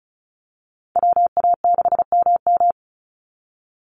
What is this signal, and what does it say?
Morse code, heard as a keyed tone, about 34 wpm, 700 Hz. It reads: WA6MM
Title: NA SOTA Chaser Callsigns 35wpm